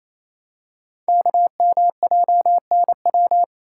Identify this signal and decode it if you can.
Morse code, heard as a keyed tone, about 28 wpm, 700 Hz. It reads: KMJNW
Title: Code Groups - 5 Letters - 28wpm